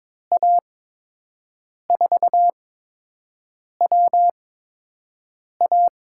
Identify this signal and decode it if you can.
Morse code, heard as a keyed tone, about 22 wpm, spaced as for 6 wpm, 700 Hz. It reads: A4WA
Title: USA Call Signs - 8x Character Spacing - 22wpm